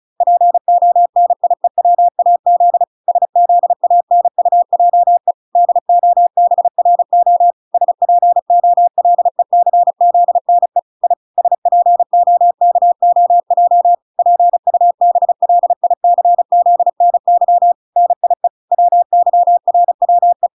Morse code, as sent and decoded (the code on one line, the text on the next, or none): .--. --- -. .. .-- .- --.. / ... --.. .- -. ..- .--- . / -.. --- -... .-. --- / ... .--. --- .-.. -.-. --.. -. / .. / ... .--. --- -.- --- .--- / .--. ..- -... .-.. .. -.-. --.. -. -.-- / -. .. / .-- -.-- .-. .-- .
PONIEWAZ SZANUJE DOBRO SPOLECZNE I SPOKOJ PUBLICZNY NIE WYRWE